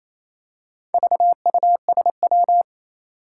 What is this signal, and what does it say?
Morse code, heard as a keyed tone, about 27 wpm, 700 Hz. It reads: VUSW
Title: Four Challenging Letter-Number Combinations 28wpm